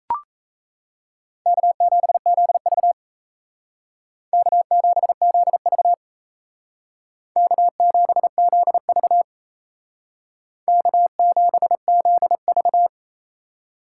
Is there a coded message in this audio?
Yes: Morse code from 1.5 to 12.9 s, heard as a keyed tone, about 36 wpm, 700 Hz. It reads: K7ZV K7ZV K7ZV K7ZV